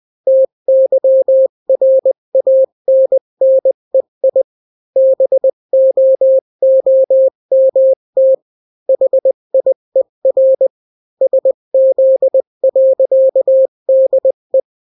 Morse code, heard as a keyed tone, about 20 wpm, 530 Hz. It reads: TYRANNEI BOOMT HIER SZ.DE
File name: daily / 18wpm